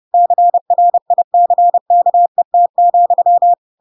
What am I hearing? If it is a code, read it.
Morse code, heard as a keyed tone, about 30 wpm, 700 Hz. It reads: CRICKET,
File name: Pinocchio / Pinocchio ch4 30 30 ik1hge